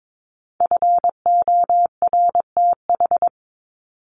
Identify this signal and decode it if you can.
Morse code, heard as a keyed tone, about 22 wpm, 700 Hz. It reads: FORTH
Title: Top 1751-2000 Words 22wpm